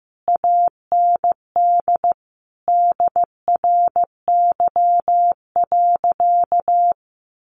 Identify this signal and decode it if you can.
Morse code, heard as a keyed tone, about 15 wpm, 700 Hz. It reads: AND DRY.
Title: Sentences from Top 1500 Words 15wpm